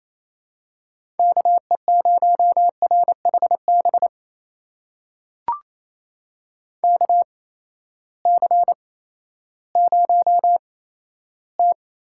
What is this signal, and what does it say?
Morse code, heard as a keyed tone, about 28 wpm, spaced as for 14 wpm, 700 Hz. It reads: KE0RHB KC0T
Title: USA Call Signs - 6 Characters - 8x Character Spacing - 28wpm